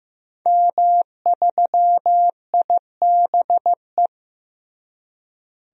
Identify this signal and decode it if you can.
Morse code, heard as a keyed tone, about 16 wpm, 700 Hz. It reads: M3IBE
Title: DXCC-100 Call Signs - 5 Characters - Repeated 3x - 15wpm